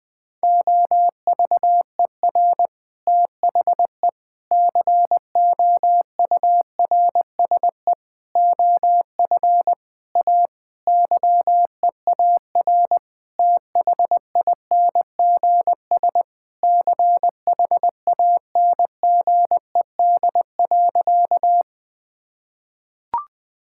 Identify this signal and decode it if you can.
Morse code, heard as a keyed tone, about 21 wpm, 700 Hz. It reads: OVER THE COURSE OF A YEAR THINGS CHANGED.
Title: Sentences with 4-Word Phrases - Spoken Beginning - 20wpm